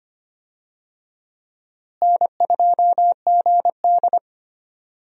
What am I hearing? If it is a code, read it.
Morse code, heard as a keyed tone, about 25 wpm, 700 Hz. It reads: N2GD
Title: USA Call Signs - 4 Characters - Repeated 2x - 25wpm